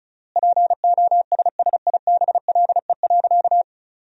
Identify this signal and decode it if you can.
Morse code, heard as a keyed tone, about 36 wpm, 700 Hz. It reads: POSSIBLE.
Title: 3x Word-Spacing - Sentences from Top 500 Words - 35wpm